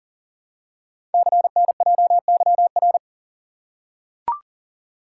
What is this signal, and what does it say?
Morse code, heard as a keyed tone, about 40 wpm, 700 Hz. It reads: CNJYR
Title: Code Groups - 5 Letters - 40wpm